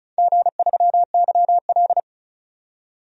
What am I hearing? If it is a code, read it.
Morse code, heard as a keyed tone, about 36 wpm, 700 Hz. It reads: G3YL
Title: DXCC-100 Call Signs - 4 Characters - 35wpm